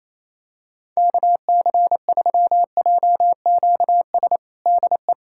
Morse code, heard as a keyed tone, about 27 wpm, 700 Hz. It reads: KC3JQS DE
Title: USA Callsign DE Callsign 28wpm